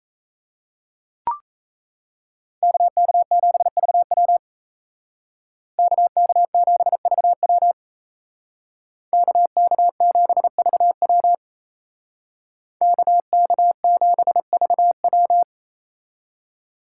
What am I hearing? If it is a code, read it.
Morse code, heard as a keyed tone, about 34 wpm, 700 Hz. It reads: KK7VW KK7VW KK7VW KK7VW
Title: Speed Racing - USA 2x2 Call Signs 28wpm